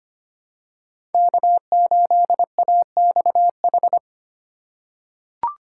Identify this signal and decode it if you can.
Morse code, heard as a keyed tone, about 25 wpm, 700 Hz. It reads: K8AXH
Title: USA Call Signs - 5 Characters - 2x Character Spacing - 25wpm